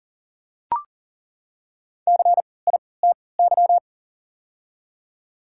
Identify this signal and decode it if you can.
Morse code, heard as a keyed tone, about 40 wpm, spaced as for 23 wpm, 700 Hz. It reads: CITY
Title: 3x Character Spacing - Top 500 Words - 40wpm